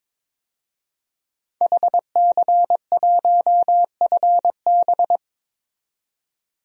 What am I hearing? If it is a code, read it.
Morse code, heard as a keyed tone, about 23 wpm, 700 Hz. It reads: HC1FB
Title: DXCC-100 Call Signs - 5 Characters - 22wpm